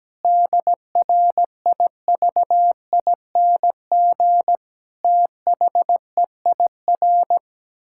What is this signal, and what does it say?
Morse code, heard as a keyed tone, about 17 wpm, 700 Hz. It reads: DRIVING THEIR
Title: Pangram Challenge v2 - 17wpm